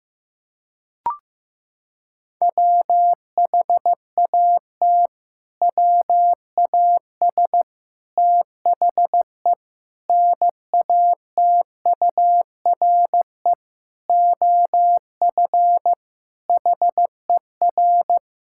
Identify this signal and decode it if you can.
Morse code, heard as a keyed tone, about 16 wpm, 700 Hz. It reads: WHAT WAS THE NATURE OF HER
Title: Sentences from Top 1000 Words - Review Part 6 - 15wpm